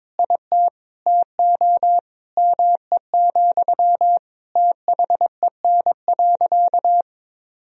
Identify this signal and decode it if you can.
Morse code, heard as a keyed tone, about 21 wpm, 700 Hz. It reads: IT TO ME, THEN.
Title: Sentences from Top 200 Words and Abbreviations 22wpm